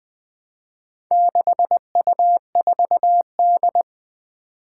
Morse code, heard as a keyed tone, about 20 wpm, 700 Hz. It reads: 6U4D
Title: Four Challenging Letter-Number Combinations 20wpm